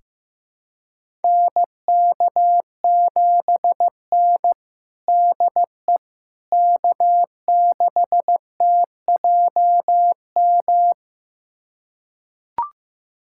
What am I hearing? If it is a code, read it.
Morse code, heard as a keyed tone, about 15 wpm, 700 Hz. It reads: NK7N DE K6TJM
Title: USA Callsign DE Callsign 15wpm